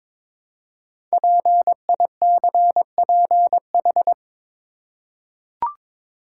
Morse code, heard as a keyed tone, about 23 wpm, 700 Hz. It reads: PICPH
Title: Code Groups - 5 Letters - 22wpm